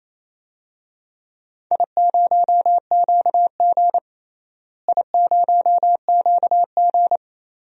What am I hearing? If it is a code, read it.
Morse code, heard as a keyed tone, about 28 wpm, 700 Hz. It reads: I0QG I0QG
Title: DXCC-100 Call Signs - 4 Characters - Repeated 2x - 28wpm